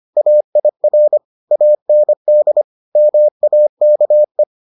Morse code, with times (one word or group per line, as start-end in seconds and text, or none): 0.2-1.2 AIR
1.5-2.6 AND
3.0-4.4 MAKE